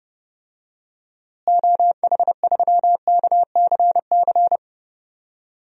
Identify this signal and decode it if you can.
Morse code, heard as a keyed tone, about 30 wpm, 700 Hz. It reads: OH3KCC
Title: DXCC-100 Call Signs - 6 Characters - Repeated 3x - 30wpm